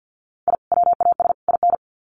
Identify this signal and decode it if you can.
Morse code, heard as a keyed tone, about 50 wpm, 700 Hz. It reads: I WAS IN
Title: Sets of 3 Words 50wpm